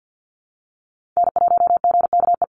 Morse code, heard as a keyed tone, about 52 wpm, 700 Hz. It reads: N1GKE
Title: Call Signs Repeated 3x - 50wpm